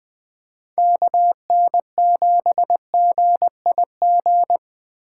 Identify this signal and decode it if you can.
Morse code, heard as a keyed tone, about 20 wpm, 700 Hz. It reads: KN7GIG
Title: Speed Racing - USA 2x3 Call Signs 17wpm